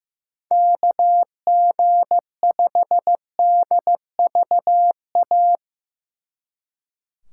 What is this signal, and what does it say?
Morse code, heard as a keyed tone, about 17 wpm, 700 Hz. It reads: KG5DVA
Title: Speed Racing - USA 2x3 Call Signs 15wpm